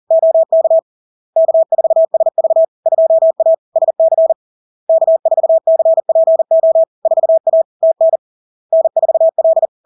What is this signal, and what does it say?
Morse code, heard as a keyed tone, about 40 wpm, 650 Hz. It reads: OK K4SV 2A SC K4CPO 4A TN N4L
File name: LARK-FD-2022-Log-in-CW-40wpm-650-Hz